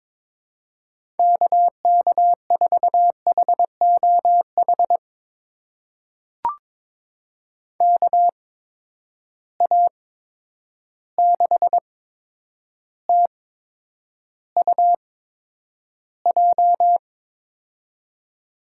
Morse code, heard as a keyed tone, about 22 wpm, spaced as for 9 wpm, 700 Hz. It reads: KK4HOH KA6TUJ
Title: USA Call Signs - 6 Characters - 8x Character Spacing - 22wpm